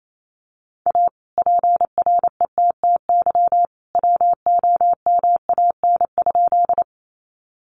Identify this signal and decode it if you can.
Morse code, heard as a keyed tone, about 28 wpm, 700 Hz. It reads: A PRETTY WOMAN?
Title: Sentences from Top 1100 Words 28wpm